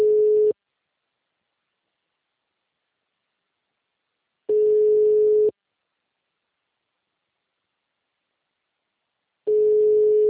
Ringback tone (Europe, 425 Hz).
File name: out- -2030- - - .7861